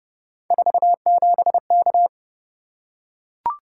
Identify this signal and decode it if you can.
Morse code, heard as a keyed tone, about 28 wpm, 700 Hz. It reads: 47K